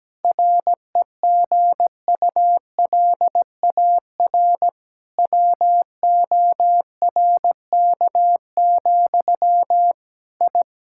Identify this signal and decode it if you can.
Morse code, heard as a keyed tone, about 17 wpm, 700 Hz. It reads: REGULAR WORK, I
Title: Sentences from Top 1000 Words - Review Part 6 - 17wpm